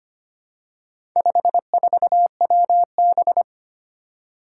Morse code, heard as a keyed tone, about 25 wpm, 700 Hz. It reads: 54WB
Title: Four Challenging Letter-Number Combinations 25wpm